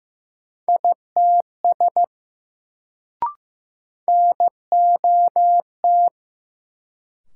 Morse code, heard as a keyed tone, about 15 wpm, 700 Hz. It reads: ITS NOT
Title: Sets of 2 Words - New Word Drill - 15wpm